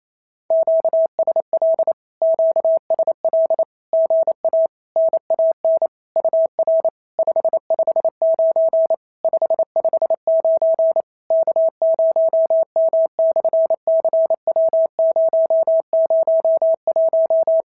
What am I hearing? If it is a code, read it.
Morse code, heard as a keyed tone, about 27 wpm, 650 Hz. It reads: QSL QSL GA NAN UR 559 559 K0M/CW001
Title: SOTA QSO Practice 28wpm